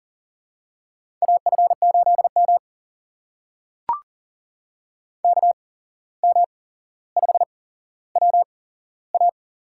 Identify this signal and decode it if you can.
Morse code, heard as a keyed tone, about 40 wpm, spaced as for 16 wpm, 700 Hz. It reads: AF8M KM5WA
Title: USA Call Signs - 8x Character Spacing - 40wpm